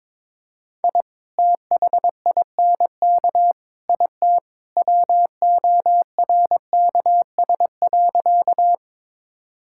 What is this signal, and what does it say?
Morse code, heard as a keyed tone, about 22 wpm, 700 Hz. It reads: I THINK IT WORKS.
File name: practice / 22wpm / Sentences with 2-Word Phrases - 22wpm